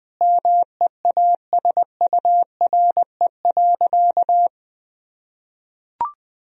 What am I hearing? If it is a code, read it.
Morse code, heard as a keyed tone, about 20 wpm, 700 Hz. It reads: MEASURE.
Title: Sentences from Top 1000 Words - Review Part 3 - 20wpm